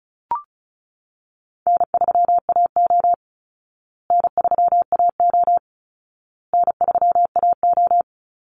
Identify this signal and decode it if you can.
Morse code, heard as a keyed tone, about 34 wpm, 700 Hz. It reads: N3AO N3AO N3AO
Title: Easy Call Signs Repeated 3x - 35wpm